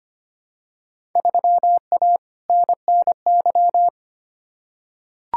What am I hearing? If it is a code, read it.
Morse code, heard as a keyed tone, about 25 wpm, 700 Hz. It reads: 3A NNY
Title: Field Day Exchanges - 25wpm